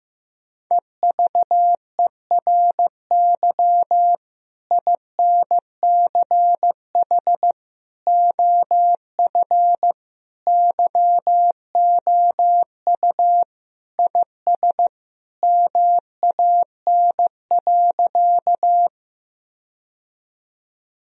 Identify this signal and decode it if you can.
Morse code, heard as a keyed tone, about 16 wpm, 700 Hz. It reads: EVERY INCH OF YOU IS MAN.
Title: Sentences from Top 1000 Words - Review Part 3 - 15wpm